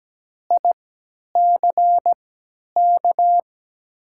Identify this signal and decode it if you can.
Morse code, heard as a keyed tone, about 17 wpm, spaced as for 10 wpm, 700 Hz. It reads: ICK
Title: 3x Character Spacing - Top 500 Words - 17wpm